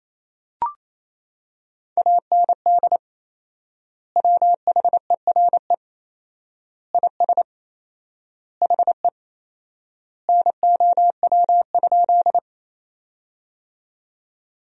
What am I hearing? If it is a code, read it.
Morse code, heard as a keyed tone, about 28 wpm, 700 Hz. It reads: AND WHERE IS HE NOW?
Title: 4x Word-Spacing - Sentences from Top 200 Words - 28wpm